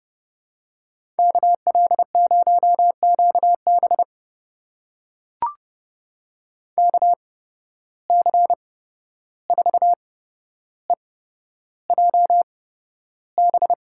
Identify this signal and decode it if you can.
Morse code, heard as a keyed tone, about 30 wpm, spaced as for 12 wpm, 700 Hz. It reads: KL0QB KC4EJB
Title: USA Call Signs - 8x Character Spacing - 30wpm